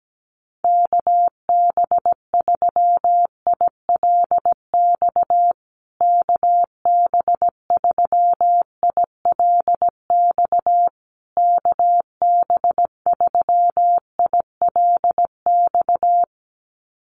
Morse code, heard as a keyed tone, about 17 wpm, 700 Hz. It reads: KB3ILX KB3ILX KB3ILX
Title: USA Call Signs Repeated 3x 17wpm